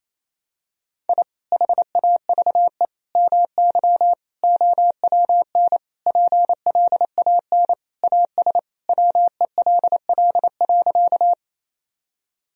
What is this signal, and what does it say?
Morse code, heard as a keyed tone, about 28 wpm, 700 Hz. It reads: I HAVE MY OWN PLAN AS WELL.
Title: Sentences from Top 1000 Words - Review Part 2 - 28wpm